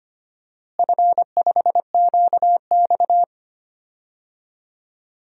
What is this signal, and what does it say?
Morse code, heard as a keyed tone, about 25 wpm, 700 Hz. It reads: F5QX
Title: DXCC-100 Call Signs - 4 Characters - Repeated 2x - 25wpm